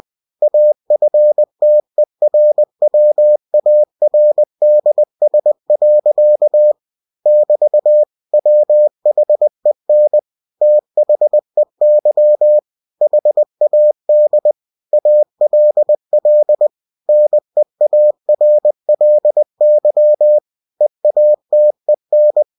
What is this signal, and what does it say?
Morse code, heard as a keyed tone, about 20 wpm, 600 Hz. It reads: AFTERWARDS. <BT> WHEN THEY HAD ALL NEARLY EATEN
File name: cw-books / winnie-the-pooh / winnie-20wpm-page-0107